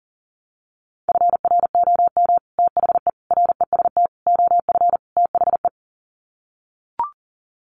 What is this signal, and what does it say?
Morse code, heard as a keyed tone, about 40 wpm, 700 Hz. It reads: FROM THE REST OF THE
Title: Sets of 5 Words 40wpm